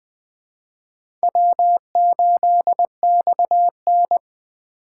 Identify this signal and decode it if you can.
Morse code, heard as a keyed tone, about 20 wpm, 700 Hz. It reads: W8XN
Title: USA Call Signs - 4 Characters - 2x Character Spacing - 20wpm